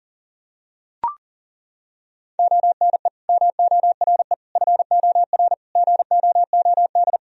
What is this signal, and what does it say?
Morse code, heard as a keyed tone, about 40 wpm, 700 Hz. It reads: ONE MORE FOR GOOD
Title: Sentences from Top 1000 Words - Review Part 2 - 40wpm